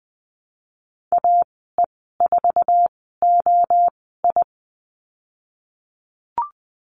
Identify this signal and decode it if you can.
Morse code, heard as a keyed tone, about 20 wpm, spaced as for 15 wpm, 700 Hz. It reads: AE4OI
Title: 048 Lesson 9g - Number 4 - Call Signs 20wpm